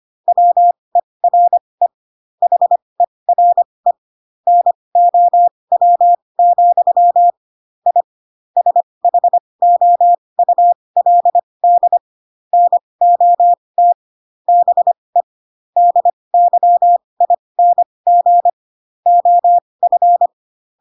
Morse code, heard as a keyed tone, about 25 wpm, spaced as for 20 wpm, 700 Hz. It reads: WERE HERE NOW, I SHOULD NOT BE DYING OF